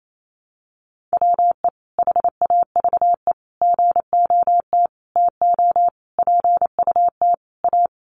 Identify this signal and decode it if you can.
Morse code, heard as a keyed tone, about 28 wpm, 700 Hz. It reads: WE HAVE GOT TO PUT A
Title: Sentences from Top 1200 Words 28wpm